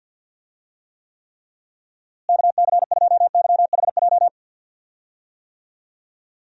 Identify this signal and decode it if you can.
Morse code, heard as a keyed tone, about 47 wpm, 700 Hz. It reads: KC2YHJ
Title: USA Call Signs - 6 Characters - Repeated 3x - 50wpm